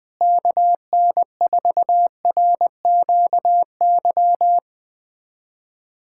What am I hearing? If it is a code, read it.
Morse code, heard as a keyed tone, about 20 wpm, 700 Hz. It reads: KN4RQY
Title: USA Callsign DE Callsign - 6 Characters - 20wpm